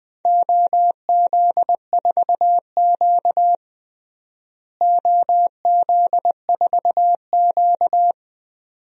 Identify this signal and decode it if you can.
Morse code, heard as a keyed tone, about 20 wpm, 700 Hz. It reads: OZ4Q OZ4Q
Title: DXCC-100 Call Signs - 4 Characters - Repeated 2x - 20wpm